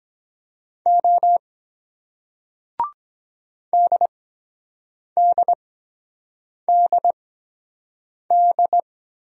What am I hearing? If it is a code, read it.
Morse code, heard as a keyed tone, about 23 wpm, spaced as for 7 wpm, 700 Hz. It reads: O DDDD